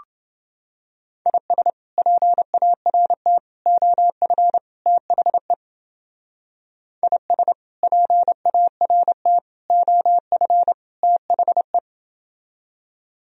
Morse code, heard as a keyed tone, about 29 wpm, 700 Hz. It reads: IS PART OF THE IS PART OF THE